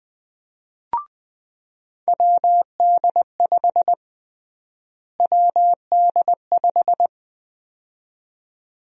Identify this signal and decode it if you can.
Morse code, heard as a keyed tone, about 20 wpm, 700 Hz. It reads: WD5 WD5
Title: USA Call Sign Prefixes - Repeated 2x - 20wpm